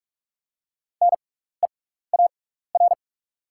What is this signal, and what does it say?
Morse code, heard as a keyed tone, about 48 wpm, spaced as for 16 wpm, 700 Hz. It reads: NEAR
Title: 6x Character Spacing - Top 200 Words - 45wpm